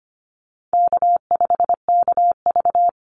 Morse code, heard as a keyed tone, about 25 wpm, 700 Hz. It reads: K5KV
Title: Call Signs Repeated 2x - 25wpm